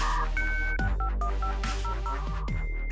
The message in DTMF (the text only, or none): 651500